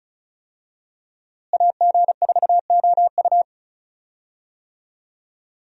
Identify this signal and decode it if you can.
Morse code, heard as a keyed tone, about 36 wpm, 700 Hz. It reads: AG4OU
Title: USA Call Signs - 5 Characters - Repeated 3x - 35wpm